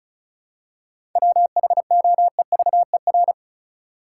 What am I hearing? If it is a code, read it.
Morse code, heard as a keyed tone, about 34 wpm, 700 Hz. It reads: WHOEVER
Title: Seven Letter Words - 35wpm